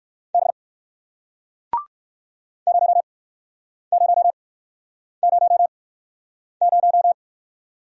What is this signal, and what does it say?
Morse code, heard as a keyed tone, about 61 wpm, spaced as for 9 wpm, 700 Hz. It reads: B 0000